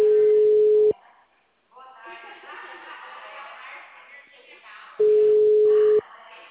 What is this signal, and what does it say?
Ringback tone (Europe, 425 Hz), with speech.